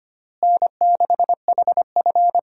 Morse code, heard as a keyed tone, about 25 wpm, 700 Hz. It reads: N6HF
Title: USA Call Signs - 4 Characters - Repeated 2x - 25wpm